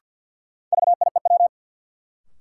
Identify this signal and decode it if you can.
Morse code, heard as a keyed tone, about 51 wpm, 700 Hz. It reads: VIEW
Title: Sets of 5 Words - New Word Drill - 50wpm